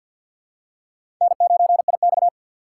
Morse code, heard as a keyed tone, about 53 wpm, 700 Hz. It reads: N9IX
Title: USA Call Signs - 4x Character Spacing - 50wpm